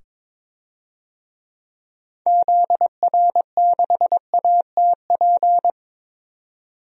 Morse code, heard as a keyed tone, about 22 wpm, 700 Hz. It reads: ZR6ATP
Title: DXCC-100 Call Signs - 6 Characters - 22wpm